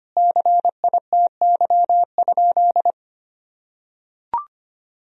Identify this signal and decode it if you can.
Morse code, heard as a keyed tone, about 25 wpm, 700 Hz. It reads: CITY?
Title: Sentences from Top 1000 Words - Review Part 2 - 25wpm